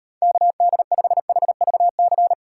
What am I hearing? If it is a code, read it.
Morse code, heard as a keyed tone, about 32 wpm, 700 Hz. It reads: KD5HVC